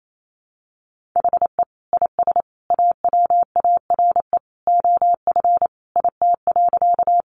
Morse code, heard as a keyed tone, about 28 wpm, 700 Hz. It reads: HE IS AWARE OF IT.
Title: Sentences from Top 1200 Words 28wpm